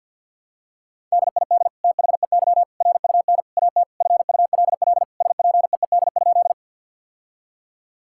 Morse code, heard as a keyed tone, about 50 wpm, 700 Hz. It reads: DID THEY RUN AT FULL SPEED?
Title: Sentences from Top 1000 Words - Review Part 5 - 50wpm